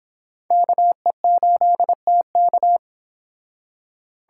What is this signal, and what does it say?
Morse code, heard as a keyed tone, about 26 wpm, 700 Hz. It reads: KE8TK